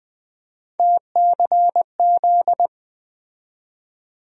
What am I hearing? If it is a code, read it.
Morse code, heard as a keyed tone, about 20 wpm, 700 Hz. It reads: TCZ
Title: USA Call Sign Postfixes - 3 Letters 20wpm